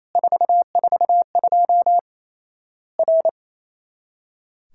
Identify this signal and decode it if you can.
Morse code, heard as a keyed tone, about 28 wpm, 700 Hz. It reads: 442 R
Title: Worked All Europe DX Contest - QTC Exchange - 28wpm